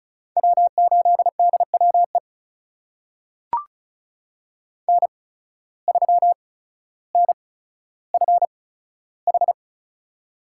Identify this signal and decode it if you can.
Morse code, heard as a keyed tone, about 34 wpm, spaced as for 15 wpm, 700 Hz. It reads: W8DWE N3NFH
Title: USA Call Signs - 5 Characters - 8x Character Spacing - 35wpm